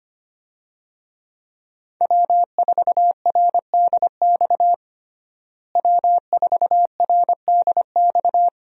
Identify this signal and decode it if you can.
Morse code, heard as a keyed tone, about 25 wpm, 700 Hz. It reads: W4RDX W4RDX
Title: USA Call Signs - 5 Characters - Repeated 2x - 25wpm